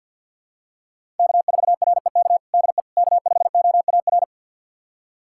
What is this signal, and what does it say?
Morse code, heard as a keyed tone, about 50 wpm, 700 Hz. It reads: K4REK DE K5OAR
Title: USA Callsign DE Callsign - 5 Characters - 50wpm